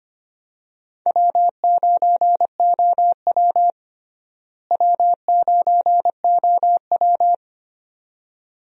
Morse code, heard as a keyed tone, about 25 wpm, 700 Hz. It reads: W9OW W9OW
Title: USA Call Signs - 4 Characters - Repeated 2x - 25wpm